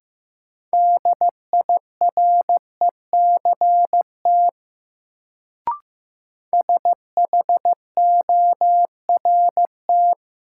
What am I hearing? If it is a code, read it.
Morse code, heard as a keyed tone, about 15 wpm, 700 Hz. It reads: DIRECT SHORT
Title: Sentences from Top 400 Words - New Word Drill - 15wpm